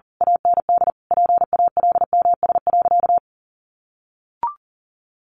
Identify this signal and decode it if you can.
Morse code, heard as a keyed tone, about 40 wpm, 700 Hz. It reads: AND PALMS.
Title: A Sound Of Thunder 40wpm